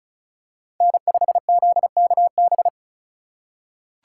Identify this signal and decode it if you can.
Morse code, heard as a keyed tone, about 34 wpm, 700 Hz. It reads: N5ZKB
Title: USA Call Signs - 5 Characters - 2x Character Spacing - 35wpm